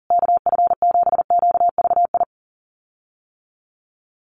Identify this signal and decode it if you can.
Morse code, heard as a keyed tone, about 40 wpm, 700 Hz. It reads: KF7QVI
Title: USA Call Signs Repeated 3x 40wpm